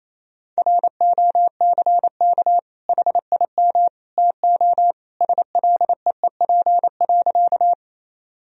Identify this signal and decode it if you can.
Morse code, heard as a keyed tone, about 28 wpm, 700 Hz. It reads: ROCK HIM TO SLEEP.
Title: Sentences from Top 1000 Words - Review Part 2 - 28wpm